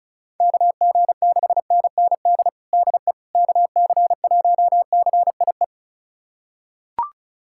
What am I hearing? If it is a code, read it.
Morse code, heard as a keyed tone, about 36 wpm, 700 Hz. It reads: KG6NND DE KC1CIE
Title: USA Callsign DE Callsign - 6 Characters - 35wpm